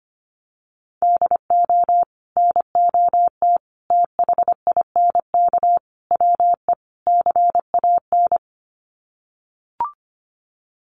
Morse code, heard as a keyed tone, about 25 wpm, 700 Hz. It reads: DO NOT THINK WE CAN
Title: Sets of 5 Words Encore2 25wpm